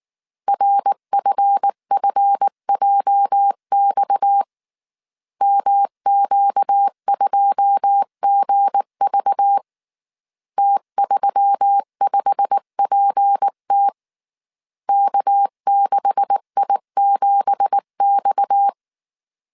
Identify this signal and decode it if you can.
Morse code, heard as a keyed tone, about 19 wpm, 800 Hz. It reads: RFFJX MQ2GV T35PT K6I7X
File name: CW / DET / 1 9 1500C S B 18wpm 2551